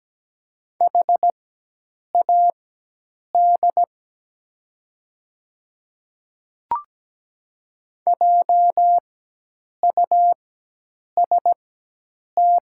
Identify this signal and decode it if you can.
Morse code, heard as a keyed tone, about 17 wpm, spaced as for 8 wpm, 700 Hz. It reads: HAD JUST
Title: 4x Character Spacing - Top 300 Words - 17wpm